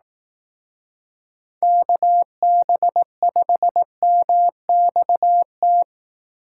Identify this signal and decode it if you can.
Morse code, heard as a keyed tone, about 18 wpm, 700 Hz. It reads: KB5MXT
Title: Speed Racing - USA Call Signs 15wpm